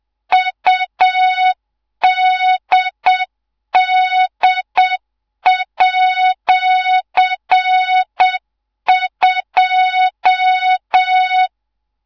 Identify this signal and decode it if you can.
Morse code, heard as a keyed tone, about 7 wpm, 3700 Hz, with noise. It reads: UDD@2
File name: Morze-HA2MN / hangp / 2 03 GYAKORLAT